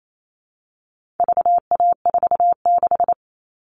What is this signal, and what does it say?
Morse code, heard as a keyed tone, about 28 wpm, 700 Hz. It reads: VA46
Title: All Cut Numbers - 4-Digit 28wpm